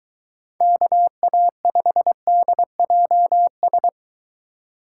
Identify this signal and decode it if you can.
Morse code, heard as a keyed tone, about 23 wpm, 700 Hz. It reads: KA5DJS